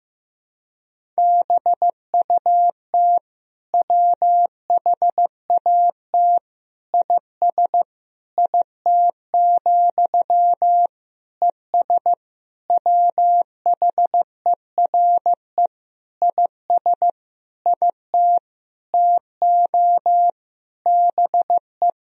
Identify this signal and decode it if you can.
Morse code, heard as a keyed tone, about 15 wpm, 700 Hz. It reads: BUT WHAT IS IT, ES WHERE IS IT TO BE
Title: Sentences from Top 300 Words and Abbreviations 15wpm